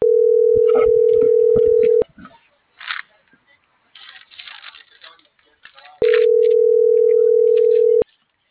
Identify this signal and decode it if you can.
Ringback tone (North America), 440 + 480 Hz.